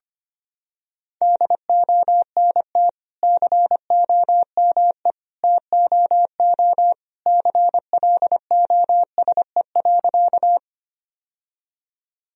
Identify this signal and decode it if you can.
Morse code, heard as a keyed tone, about 25 wpm, 700 Hz. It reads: DONT COME TOO CLOSE.
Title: Sentences from Top 1000 Words - Review Part 1 - 25wpm